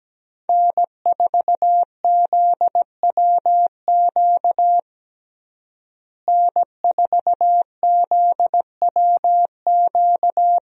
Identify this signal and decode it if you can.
Morse code, heard as a keyed tone, about 17 wpm, 700 Hz. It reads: N4ZWQ N4ZWQ
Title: USA Call Signs - 5 Characters - Repeated 3x - 17wpm